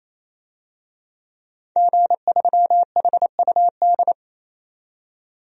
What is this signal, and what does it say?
Morse code, heard as a keyed tone, about 27 wpm, 700 Hz. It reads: G3HUD
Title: DXCC-100 Call Signs - 5 Characters - 28wpm